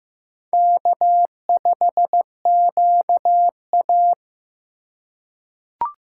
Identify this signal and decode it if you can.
Morse code, heard as a keyed tone, about 15 wpm, 700 Hz. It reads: K5QA
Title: USA Call Signs - 4 Characters - 4x Character Spacing - 15wpm